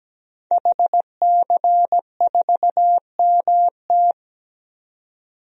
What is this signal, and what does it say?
Morse code, heard as a keyed tone, about 17 wpm, 700 Hz. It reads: HC4MT
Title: DXCC-100 Call Signs - 5 Characters - 17wpm